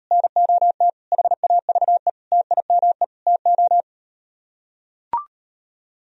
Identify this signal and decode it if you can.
Morse code, heard as a keyed tone, about 38 wpm, 700 Hz. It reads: NOT HAVE TIME TO